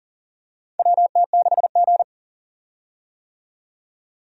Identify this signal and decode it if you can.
Morse code, heard as a keyed tone, about 40 wpm, 700 Hz. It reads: WT6G
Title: USA Call Signs - 4 Characters - 40wpm